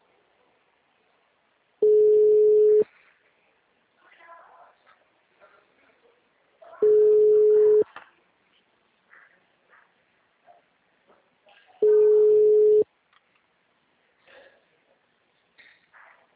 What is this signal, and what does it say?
Ringback tone (Europe, 425 Hz), with speech.
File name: out- -2017- - - .239